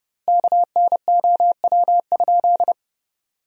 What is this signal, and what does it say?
Morse code, heard as a keyed tone, about 30 wpm, 700 Hz. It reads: KNOW?
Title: 2x Word-Spacing - Sentences from Top 500 Words - 30wpm